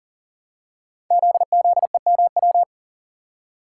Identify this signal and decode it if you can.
Morse code, heard as a keyed tone, about 40 wpm, 700 Hz. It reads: ZZEMW
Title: Code Groups - 5 Letters - 40wpm